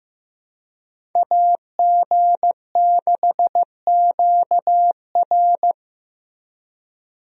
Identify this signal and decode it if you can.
Morse code, heard as a keyed tone, about 15 wpm, 700 Hz. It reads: AG6QR
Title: USA Call Signs - 5 Characters - Repeated 3x - 15wpm